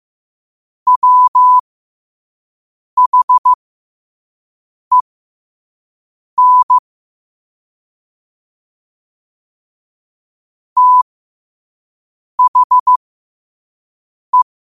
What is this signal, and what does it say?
Morse code, heard as a keyed tone, about 15 wpm, spaced as for 5 wpm, 1000 Hz. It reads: WHEN THE
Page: New quotes every day in morse code at 5 Words per minute.